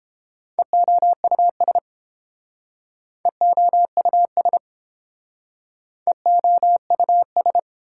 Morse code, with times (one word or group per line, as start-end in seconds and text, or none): 0.6-1.8 EOUS
3.3-4.6 EOUS
6.1-7.6 EOUS